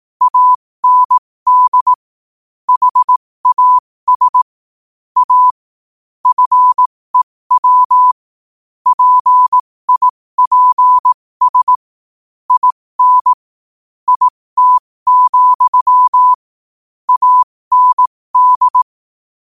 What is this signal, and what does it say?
Morse code, heard as a keyed tone, about 18 wpm, spaced as for 15 wpm, 1000 Hz. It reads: AND HAS A FEW PIPS IN IT, AND